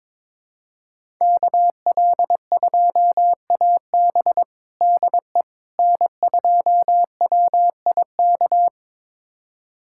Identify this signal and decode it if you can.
Morse code, heard as a keyed tone, about 22 wpm, 700 Hz. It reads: KL2AB DE N2WIK
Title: USA Callsign DE Callsign - 5 Characters - 22wpm